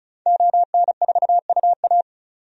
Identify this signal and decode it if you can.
Morse code, heard as a keyed tone, about 36 wpm, 700 Hz. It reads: ON4UA
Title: DXCC-100 Call Signs - 5 Characters - 35wpm